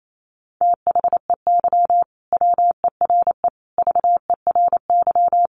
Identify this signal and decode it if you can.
Morse code, heard as a keyed tone, about 28 wpm, 700 Hz. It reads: THEY WERE VERY
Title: Sentences from Top 1000 Words 28wpm